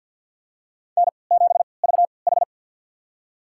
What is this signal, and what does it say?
Morse code, heard as a keyed tone, about 53 wpm, spaced as for 29 wpm, 700 Hz. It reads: N7VH
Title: USA Call Signs - 4 Characters - 3x Character Spacing - 50wpm